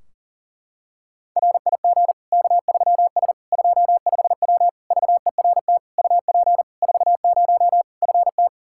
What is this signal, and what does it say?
Morse code, heard as a keyed tone, about 40 wpm, 700 Hz. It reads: RIG K3S 25W VERT UP 40 FT
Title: K1USN SST QSO Practice 40wpm